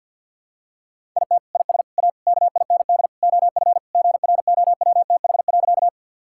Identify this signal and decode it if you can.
Morse code, heard as a keyed tone, about 46 wpm, 700 Hz. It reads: IT IS A KIND OF GROWTH.
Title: Sentences from Top 1000 Words - Review Part 5 - 50wpm